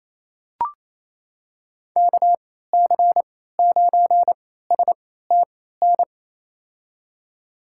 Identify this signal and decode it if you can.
Morse code, heard as a keyed tone, about 28 wpm, spaced as for 16 wpm, 700 Hz. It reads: KC9STN
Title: USA Call Signs - 6 Characters - 3x Character Spacing - 28wpm